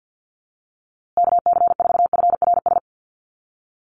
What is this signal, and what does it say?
Morse code, heard as a keyed tone, about 50 wpm, 700 Hz. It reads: KC4FRS
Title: USA Call Signs Repeated 3x 50wpm